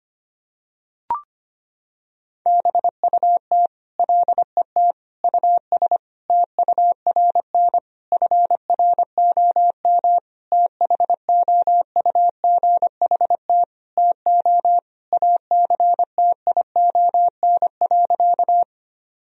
Morse code, heard as a keyed tone, about 25 wpm, 700 Hz. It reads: BUT LET US TURN FROM THOUGHT TO ACTION.
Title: Sentences from Top 1000 Words - Review Part 4 - 25wpm